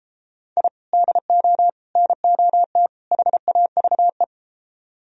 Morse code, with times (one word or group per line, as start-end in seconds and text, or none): 0.6-0.7 I
0.9-1.7 DO
2.0-2.9 NOT
3.1-4.2 HAVE